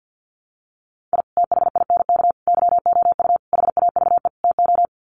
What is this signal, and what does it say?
Morse code, heard as a keyed tone, about 50 wpm, 700 Hz. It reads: I THINK YOU HAVE TO
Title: Sets of 5 Words 50wpm